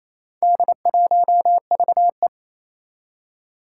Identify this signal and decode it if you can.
Morse code, heard as a keyed tone, about 28 wpm, 700 Hz. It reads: D1VE
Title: Four Challenging Letter-Number Combinations 28wpm